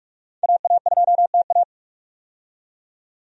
Morse code, heard as a keyed tone, about 46 wpm, 700 Hz. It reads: AA2TA
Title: USA Call Signs - 5 Characters - 45wpm